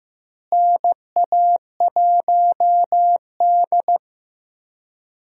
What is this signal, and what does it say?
Morse code, heard as a keyed tone, about 15 wpm, 700 Hz. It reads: NA1D
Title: USA Call Signs - 4 Characters - Repeated 3x - 15wpm